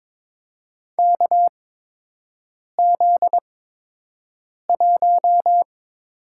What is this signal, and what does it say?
Morse code, heard as a keyed tone, about 21 wpm, spaced as for 6 wpm, 700 Hz. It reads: KZ1
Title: USA Call Signs - 4 Characters - 8x Character Spacing - 22wpm